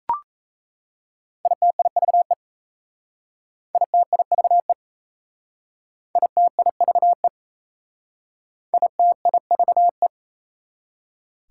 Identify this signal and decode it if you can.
Morse code, heard as a keyed tone, about 36 wpm, 690 Hz. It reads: ITIVE ITIVE ITIVE ITIVE